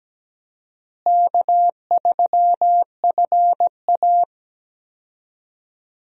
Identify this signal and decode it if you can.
Morse code, heard as a keyed tone, about 17 wpm, 700 Hz. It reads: K3FA
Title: USA Call Signs - 4 Characters - Repeated 3x - 17wpm